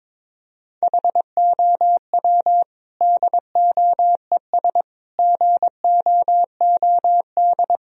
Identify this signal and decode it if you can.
Morse code, heard as a keyed tone, about 21 wpm, 700 Hz. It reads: HOW DOES GOOD
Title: Sentences from Top 1000 Words - Review Part 1 - 22wpm